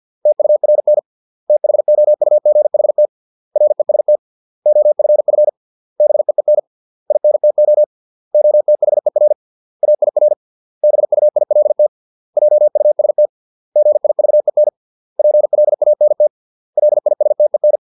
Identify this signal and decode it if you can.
Morse code, heard as a keyed tone, about 50 wpm, 600 Hz. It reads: TURN THOUGHT REST OFF BEEN INTO OTHER AIR BUILT JUST GIVEN PLANT LISTEN